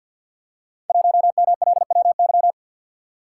Morse code, heard as a keyed tone, about 52 wpm, 700 Hz. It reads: 1MLWY
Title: Code Groups - 5 Letter:Numbers - 50wpm